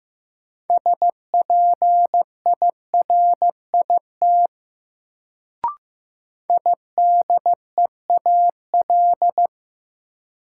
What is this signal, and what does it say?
Morse code, heard as a keyed tone, about 15 wpm, 700 Hz. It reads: SPIRIT IDEAL
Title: Sentences from Top 1400 Words - New Word Drill - 15wpm